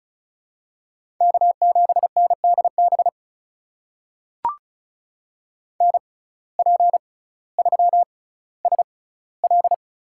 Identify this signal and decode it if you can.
Morse code, heard as a keyed tone, about 36 wpm, spaced as for 18 wpm, 700 Hz. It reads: K7NDB NP3SL
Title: USA Call Signs - 5 Characters - 6x Character Spacing - 35wpm